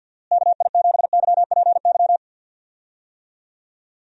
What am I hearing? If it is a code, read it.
Morse code, heard as a keyed tone, about 48 wpm, 700 Hz. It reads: KI7YPY
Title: USA Call Signs - 6 Characters - Repeated 2x - 50wpm